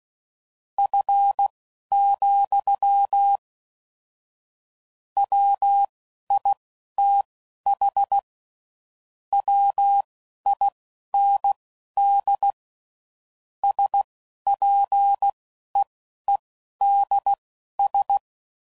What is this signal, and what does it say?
Morse code, heard as a keyed tone, about 16 wpm, spaced as for 11 wpm, 800 Hz. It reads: F, WITH WIND SPEEDS
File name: CW Code Groups @10WPM / Weather Sets / Weather 4 24